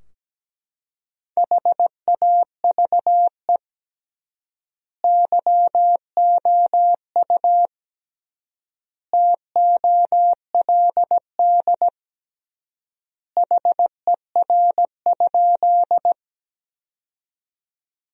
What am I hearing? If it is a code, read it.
Morse code, heard as a keyed tone, about 17 wpm, 700 Hz. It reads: HAVE YOU TOLD HER?
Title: 3x Word-Spacing - Sentences from Top 400 Words - 17wpm